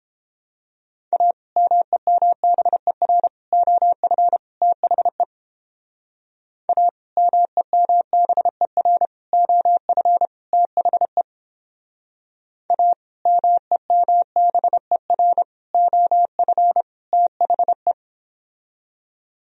Morse code, heard as a keyed tone, about 30 wpm, 700 Hz. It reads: A MEMBER OF THE A MEMBER OF THE A MEMBER OF THE